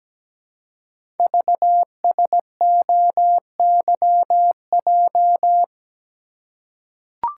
Morse code, heard as a keyed tone, about 17 wpm, 700 Hz. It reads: VSOYJ
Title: Code Groups - 5 Letters - 17wpm